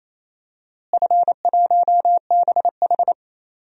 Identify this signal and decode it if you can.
Morse code, heard as a keyed tone, about 28 wpm, 700 Hz. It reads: F1BH
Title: DXCC-100 Call Signs - 4 Characters - Repeated 2x - 28wpm